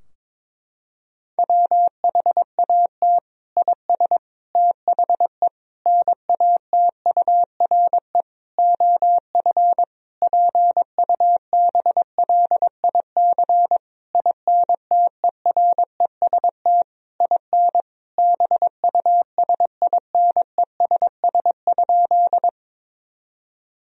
Morse code, heard as a keyed tone, about 21 wpm, 700 Hz. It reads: WHAT IS THE NATURE OF PUBLIC INTEREST IN BUSINESS?
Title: Sentences from Top 1000 Words - Review Part 3 - 22wpm